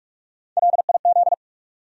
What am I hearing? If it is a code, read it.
Morse code, heard as a keyed tone, about 46 wpm, 700 Hz. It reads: LIZ
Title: USA Common Names - Max 3 Letters - 45wpm